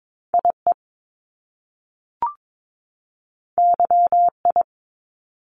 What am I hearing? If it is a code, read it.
Morse code, heard as a keyed tone, about 22 wpm, 700 Hz. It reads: IE YI
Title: Top Two Letter Combinations in Words 22wpm